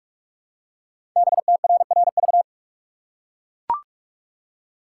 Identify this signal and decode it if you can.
Morse code, heard as a keyed tone, about 44 wpm, 700 Hz. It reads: BTRRV
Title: Code Groups - 5 Letters - 45wpm